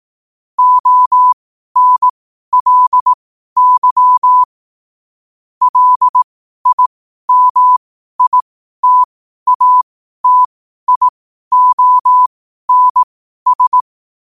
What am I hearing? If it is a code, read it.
Morse code, heard as a keyed tone, about 18 wpm, spaced as for 12 wpm, 1000 Hz. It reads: ONLY LIMITATIONS
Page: qotd-at-12-wpm